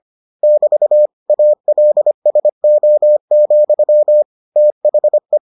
Morse code, heard as a keyed tone, about 25 wpm, 600 Hz. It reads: <BT> ALSO, THE